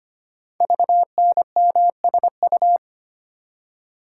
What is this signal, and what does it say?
Morse code, heard as a keyed tone, about 25 wpm, 700 Hz. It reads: VNMSU
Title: Code Groups - 5 Letters - 25wpm